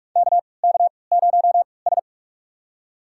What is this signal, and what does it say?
Morse code, heard as a keyed tone, about 42 wpm, spaced as for 25 wpm, 700 Hz. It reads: KK0S
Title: USA Call Signs - 4 Characters - 3x Character Spacing - 45wpm